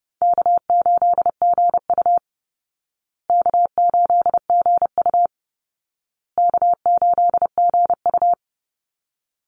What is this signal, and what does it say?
Morse code, heard as a keyed tone, about 30 wpm, 700 Hz. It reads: K8GU K8GU K8GU
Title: Easy Call Signs Repeated 3x - 30wpm